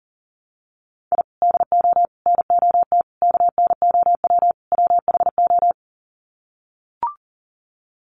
Morse code, heard as a keyed tone, about 40 wpm, 700 Hz. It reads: I DO NOT KNOW WHO
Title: Sets of 5 Words 40wpm